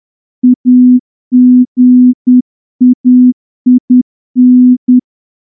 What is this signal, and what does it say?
Morse code, heard as a keyed tone, about 10 wpm, 250 Hz. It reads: AGAIN
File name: morse-training-10WPM